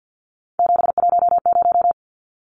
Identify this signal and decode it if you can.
Morse code, heard as a keyed tone, about 50 wpm, 700 Hz. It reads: 710
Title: Numbers - 3-Digit 50wpm